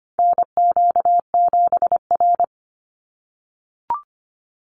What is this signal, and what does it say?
Morse code, heard as a keyed tone, about 25 wpm, 700 Hz. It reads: NQ7R
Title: NA SOTA Chaser Callsigns 25wpm